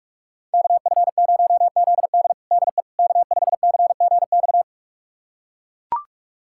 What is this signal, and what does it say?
Morse code, heard as a keyed tone, about 41 wpm, 700 Hz. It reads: KF0ZD DE K5CGX
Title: USA Callsign DE Callsign - 5 Characters - 45wpm